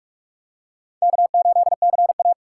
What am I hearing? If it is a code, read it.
Morse code, heard as a keyed tone, about 42 wpm, 700 Hz. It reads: K8CA